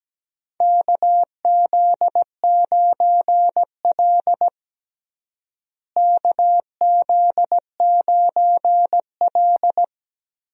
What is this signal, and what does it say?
Morse code, heard as a keyed tone, about 17 wpm, 700 Hz. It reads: KZ9L KZ9L
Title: USA Call Signs - 4 Characters - Repeated 2x - 17wpm